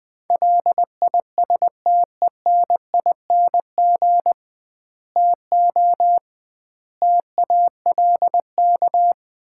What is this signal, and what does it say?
Morse code, heard as a keyed tone, about 20 wpm, 700 Hz. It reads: LISTENING TO TALK
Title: 2x Word-Spacing - Sets of 5 Words - 20wpm